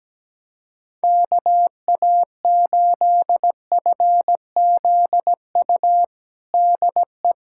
Morse code, heard as a keyed tone, about 17 wpm, 700 Hz. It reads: KA8FZU DE
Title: USA Callsign DE Callsign - 6 Characters - 17wpm